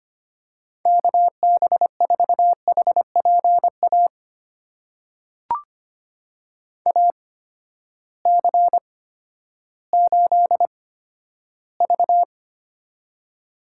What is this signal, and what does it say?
Morse code, heard as a keyed tone, about 25 wpm, spaced as for 12 wpm, 700 Hz. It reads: KB4HPA AC8V
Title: USA Call Signs - 8x Character Spacing - 25wpm